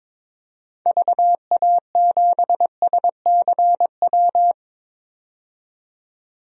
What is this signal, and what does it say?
Morse code, heard as a keyed tone, about 23 wpm, 700 Hz. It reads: VA7SCW
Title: DXCC-100 Call Signs - 6 Characters - 22wpm